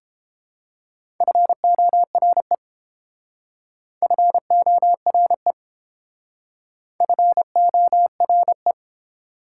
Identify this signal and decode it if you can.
Morse code, heard as a keyed tone, about 28 wpm, 700 Hz. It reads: FORE FORE FORE